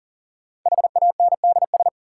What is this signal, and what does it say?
Morse code, heard as a keyed tone, about 40 wpm, 700 Hz. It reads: HANDS
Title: Sentences from Top 1000 Words - Review Part 6 - New Word Drill - 40wpm